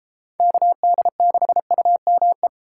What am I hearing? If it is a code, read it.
Morse code, heard as a keyed tone, about 32 wpm, 700 Hz. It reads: KD6UME